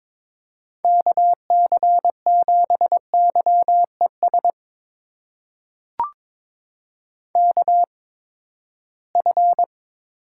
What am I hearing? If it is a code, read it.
Morse code, heard as a keyed tone, about 22 wpm, spaced as for 15 wpm, 700 Hz. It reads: KC7YES KF
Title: USA Call Signs - 6 Characters - 8x Character Spacing - 22wpm